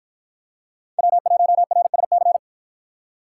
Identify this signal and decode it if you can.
Morse code, heard as a keyed tone, about 48 wpm, 700 Hz. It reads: W1RSC